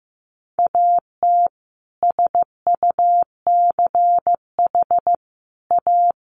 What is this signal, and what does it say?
Morse code, heard as a keyed tone, about 15 wpm, 700 Hz. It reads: AT SUCH A
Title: Sentences from Top 1000 Words 15wpm